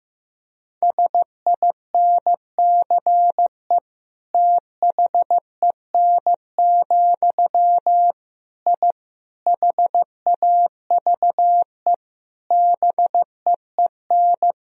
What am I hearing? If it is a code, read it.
Morse code, heard as a keyed tone, about 16 wpm, 700 Hz. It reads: SINCE THEN, I HAVE BEEN
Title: Sentences from Top 1000 Words - Review Part 3 - 15wpm